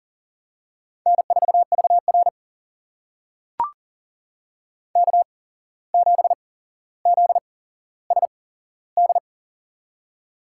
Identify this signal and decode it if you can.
Morse code, heard as a keyed tone, about 40 wpm, spaced as for 16 wpm, 700 Hz. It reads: N4VR K7ZSD
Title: USA Call Signs - 8x Character Spacing - 40wpm